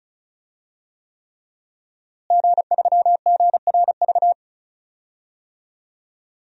Morse code, heard as a keyed tone, about 36 wpm, 700 Hz. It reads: G3GRV
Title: DXCC-100 Call Signs - 5 Characters - Repeated 3x - 35wpm